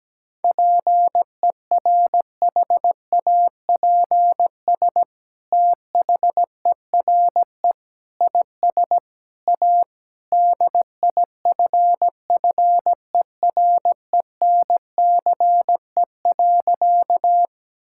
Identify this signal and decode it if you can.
Morse code, heard as a keyed tone, about 17 wpm, 700 Hz. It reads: PERHAPS THERE IS A DIFFERENCE.
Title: Sentences from Top 1000 Words - Review Part 5 - 17wpm